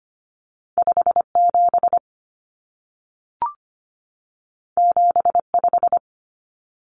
Morse code, heard as a keyed tone, about 25 wpm, 700 Hz. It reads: 57 75
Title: Common Cut Numbers - 2-Digit 25wpm